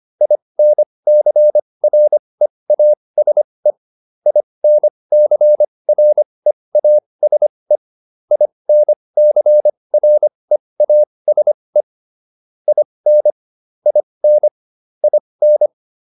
Morse code, heard as a keyed tone, about 25 wpm, spaced as for 20 wpm, 600 Hz. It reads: INCREASE INCREASE INCREASE IN IN IN
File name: IN8 20